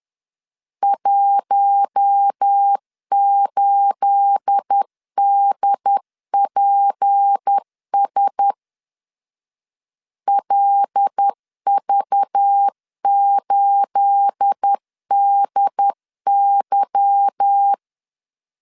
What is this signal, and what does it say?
Morse code, heard as a keyed tone, about 10 wpm, 800 Hz. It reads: 18DPS LV8DY
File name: LEZIONI / 9 28 10wpm 2842